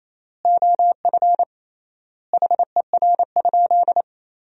Morse code, heard as a keyed tone, about 28 wpm, 700 Hz. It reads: OF HER?
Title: 3x Word-Spacing - Sentences from Top 200 Words - 28wpm